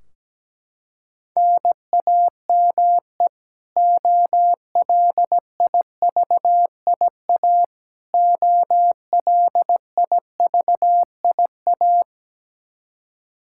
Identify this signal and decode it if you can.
Morse code, heard as a keyed tone, about 17 wpm, 700 Hz. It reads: NAME OLIVIA OLIVIA
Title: Practice QSO - Exchange 1 - Beginning - 17wpm